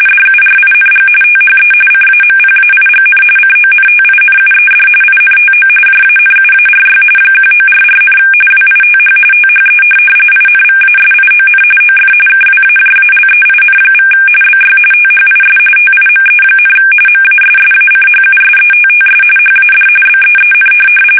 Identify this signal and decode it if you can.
RTTY, about 45.45 baud. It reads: DURING THE CAMPAIGN CONTINUED INTO JANUARY 1944, WITH THE INVASION OF KWAJALEIN= BY THIS TIME, MASSACHUSETTS HAD BEEN
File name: fsk850 / USS-Massachusetts